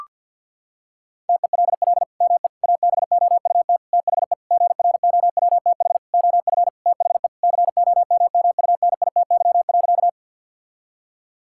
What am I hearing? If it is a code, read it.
Morse code, heard as a keyed tone, about 51 wpm, 690 Hz. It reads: TELL ME ABOUT THE GROWTH OF THE COMMUNITY.
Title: Sentences from Top 1000 Words - Review Part 5 - 50wpm